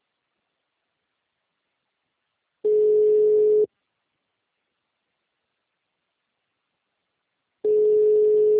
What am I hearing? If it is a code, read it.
Ringback tone (Europe, 425 Hz).